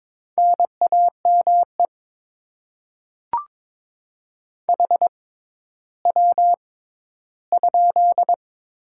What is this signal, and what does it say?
Morse code, heard as a keyed tone, about 22 wpm, spaced as for 13 wpm, 700 Hz. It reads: NAME HW?
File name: Top 20 QSO Elements - 6x Character Spacing - 22wpm